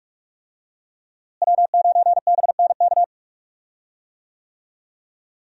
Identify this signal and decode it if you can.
Morse code, heard as a keyed tone, about 45 wpm, 700 Hz. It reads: W9BNK
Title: USA Call Signs - 5 Characters - Repeated 2x - 45wpm